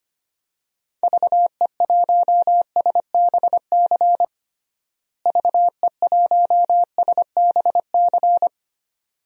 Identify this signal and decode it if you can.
Morse code, heard as a keyed tone, about 25 wpm, 700 Hz. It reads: VE1SBC VE1SBC
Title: DXCC-100 Call Signs - 6 Characters - Repeated 2x - 25wpm